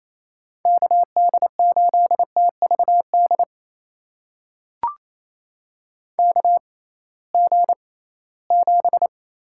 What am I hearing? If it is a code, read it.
Morse code, heard as a keyed tone, about 28 wpm, spaced as for 18 wpm, 700 Hz. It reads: KD8TVD KG7
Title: USA Call Signs - 6 Characters - 6x Character Spacing - 28wpm